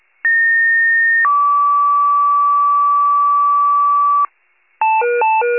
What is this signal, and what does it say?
Two-tone page, heard as a one-second tone, then a three-second tone, about 1850 then 1150 Hz, with noise. It is heard from 0.2 to 4.3 s.